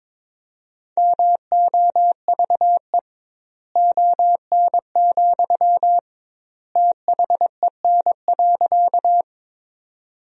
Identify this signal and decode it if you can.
Morse code, heard as a keyed tone, about 22 wpm, 700 Hz. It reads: MOVE ON, THEN.
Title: 2x Word-Spacing - Sentences from Top 200 Words - 22wpm